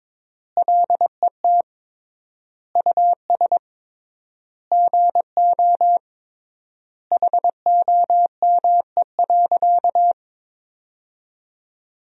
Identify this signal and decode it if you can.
Morse code, heard as a keyed tone, about 22 wpm, 700 Hz. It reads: LET US GO HOME.
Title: 3x Word-Spacing - Sentences from Top 300 Words - 22wpm